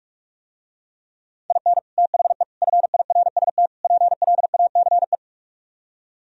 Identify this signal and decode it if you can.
Morse code, heard as a keyed tone, about 48 wpm, 700 Hz. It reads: IN THE FIRST PLACE